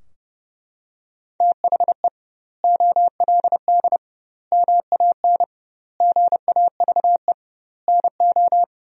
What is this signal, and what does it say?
Morse code, heard as a keyed tone, about 30 wpm, 700 Hz. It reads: THE OLD MAN GAVE NO
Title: 2x Word-Spacing - Sentences from Top 500 Words - 30wpm